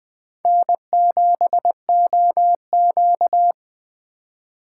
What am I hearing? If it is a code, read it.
Morse code, heard as a keyed tone, about 20 wpm, 700 Hz. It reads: N7OQ